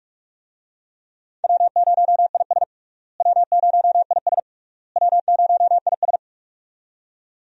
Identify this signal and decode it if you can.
Morse code, heard as a keyed tone, about 45 wpm, 700 Hz. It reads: W0IS W0IS W0IS
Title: USA Call Signs - 4 Characters - Repeated 3x - 45wpm